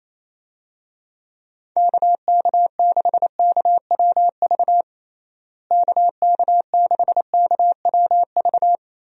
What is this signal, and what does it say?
Morse code, heard as a keyed tone, about 28 wpm, 700 Hz. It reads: KK6KWV KK6KWV
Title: USA Call Signs - 6 Characters - Repeated 3x - 28wpm